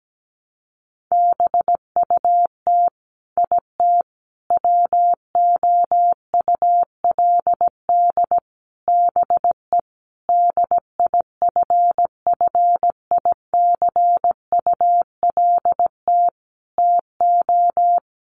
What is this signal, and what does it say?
Morse code, heard as a keyed tone, about 17 wpm, 700 Hz. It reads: BUT IT WOULD BE DIFFICULT TO
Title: Sentences from Top 1400 Words 17wpm